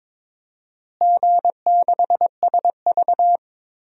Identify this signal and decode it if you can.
Morse code, heard as a keyed tone, about 22 wpm, 700 Hz. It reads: G6SV
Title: DXCC-100 Call Signs - 4 Characters - Repeated 2x - 22wpm